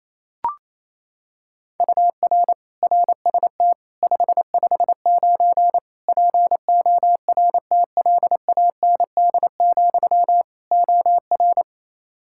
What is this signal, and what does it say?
Morse code, heard as a keyed tone, about 28 wpm, 700 Hz. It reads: UR RST 559 PORTLAND, OR
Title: K1USN SST QSO Practice 28wpm